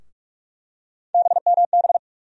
Morse code, heard as a keyed tone, about 41 wpm, 700 Hz. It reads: BMB
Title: USA Call Sign Postfixes - 3 Letters 45wpm